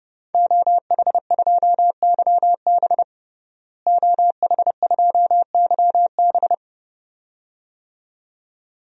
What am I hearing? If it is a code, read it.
Morse code, heard as a keyed tone, about 30 wpm, 700 Hz. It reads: OH2YB OH2YB
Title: DXCC-100 Call Signs - 5 Characters - Repeated 2x - 30wpm